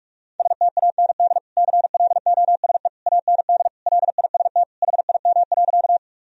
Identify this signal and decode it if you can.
Morse code, heard as a keyed tone, about 46 wpm, 700 Hz. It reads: STAND CLOSE AND LIST HIM.
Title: Sentences from Top 1000 Words - Review Part 2 - 45wpm